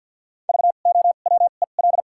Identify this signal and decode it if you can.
Morse code, heard as a keyed tone, about 52 wpm, 700 Hz. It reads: V O W E L
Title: 2x Character Spacing - Top 500 Words - 50wpm